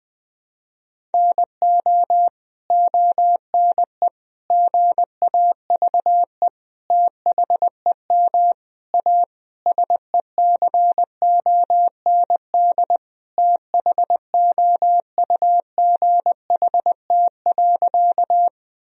Morse code, heard as a keyed tone, about 20 wpm, 700 Hz. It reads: NO ONE GAVE THEM A SECOND THOUGHT.
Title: Sentences from Top 1000 Words - Review Part 1 - 20wpm